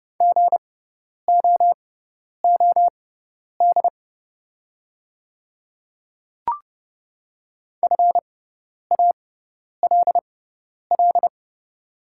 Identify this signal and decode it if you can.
Morse code, heard as a keyed tone, about 30 wpm, spaced as for 10 wpm, 700 Hz. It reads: GOOD FALL
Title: 6x Character Spacing - Top 500 Words - 30wpm